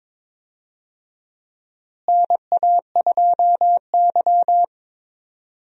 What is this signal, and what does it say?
Morse code, heard as a keyed tone, about 21 wpm, 700 Hz. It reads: NA2Y
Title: USA Call Signs - 4 Characters - Repeated 3x - 22wpm